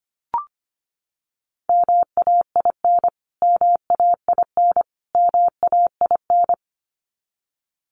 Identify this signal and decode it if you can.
Morse code, heard as a keyed tone, about 25 wpm, 700 Hz. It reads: MAIN MAIN MAIN
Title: Top 300 Words Repeated 25wpm